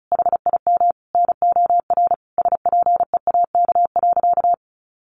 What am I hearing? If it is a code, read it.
Morse code, heard as a keyed tone, about 34 wpm, 700 Hz. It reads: HIM NOR SPEAK.
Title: Sentences from Top 1500 Words 35wpm